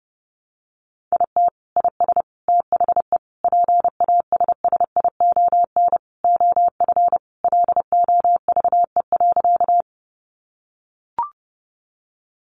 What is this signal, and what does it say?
Morse code, heard as a keyed tone, about 30 wpm, 700 Hz. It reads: IT IS THE PASSION OF LOVE.
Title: Sentences from Top 1300 Words 30wpm